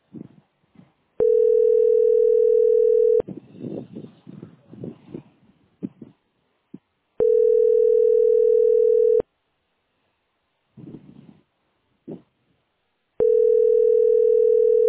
Ringback tone (North America), 440 + 480 Hz.